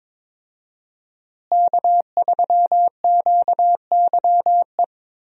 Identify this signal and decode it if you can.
Morse code, heard as a keyed tone, about 22 wpm, 700 Hz. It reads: K3QYE
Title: USA Call Signs - 5 Characters - Repeated 3x - 22wpm